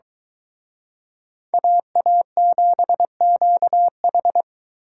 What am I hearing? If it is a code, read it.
Morse code, heard as a keyed tone, about 24 wpm, 700 Hz. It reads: AA7QH